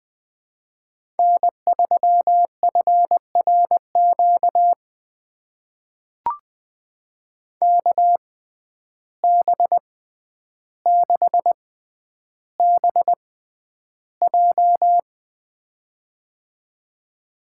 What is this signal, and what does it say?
Morse code, heard as a keyed tone, about 20 wpm, spaced as for 10 wpm, 700 Hz. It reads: N3FRQ KB6BJ
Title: USA Call Signs - 6x Character Spacing - 20wpm